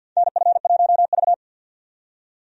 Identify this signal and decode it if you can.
Morse code, heard as a keyed tone, about 52 wpm, 700 Hz. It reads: NF1V
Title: USA Call Signs - 4 Characters - 50wpm